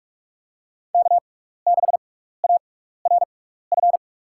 Morse code, heard as a keyed tone, about 47 wpm, spaced as for 16 wpm, 700 Hz. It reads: K6ARF
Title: USA Call Signs - 5 Characters - 6x Character Spacing - 45wpm